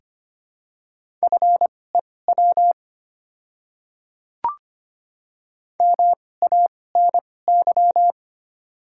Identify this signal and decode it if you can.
Morse code, heard as a keyed tone, about 25 wpm, spaced as for 18 wpm, 700 Hz. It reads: FEW MANY
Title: 2x Character Spacing - Top 300 Words - 25wpm